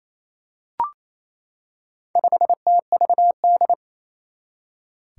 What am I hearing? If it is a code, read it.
Morse code, heard as a keyed tone, about 28 wpm, 700 Hz. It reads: 5TVD
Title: Four Challenging Letter-Number Combinations 28wpm